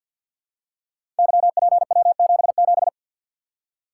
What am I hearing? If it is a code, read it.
Morse code, heard as a keyed tone, about 48 wpm, 700 Hz. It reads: YPW77
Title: Code Groups - 5 Letter:Numbers - 50wpm